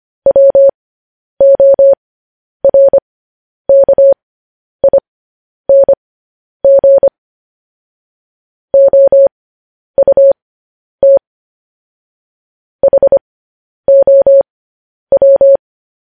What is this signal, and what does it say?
Morse code, heard as a keyed tone, about 25 wpm, spaced as for 10 wpm, 550 Hz. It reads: WORKING OUT HOW
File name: podcast / News headlines-1x-25x10